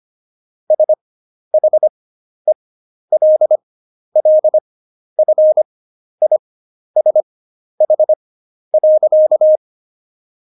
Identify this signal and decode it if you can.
Morse code, heard as a keyed tone, about 25 wpm, spaced as for 11 wpm, 620 Hz. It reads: SHELLFISH.
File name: Jokes-25-char-10-eff